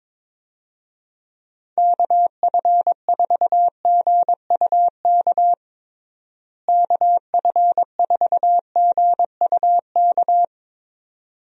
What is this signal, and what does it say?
Morse code, heard as a keyed tone, about 22 wpm, 700 Hz. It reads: KF4GUK KF4GUK
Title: USA Call Signs - 6 Characters - Repeated 2x - 22wpm